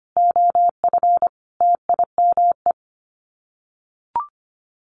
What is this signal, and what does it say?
Morse code, heard as a keyed tone, about 25 wpm, 700 Hz. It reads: OF TIME
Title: Sets of 5 Words Encore 25wpm